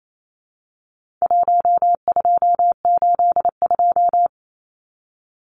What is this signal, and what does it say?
Morse code, heard as a keyed tone, about 28 wpm, 700 Hz. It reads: 1282
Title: Numbers - 4-Digit 28wpm